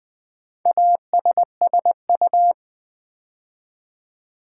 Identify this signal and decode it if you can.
Morse code, heard as a keyed tone, about 20 wpm, 700 Hz. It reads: ASSU
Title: Build A Word - 4 Letters At A Time - 10-Letter Words - 20wpm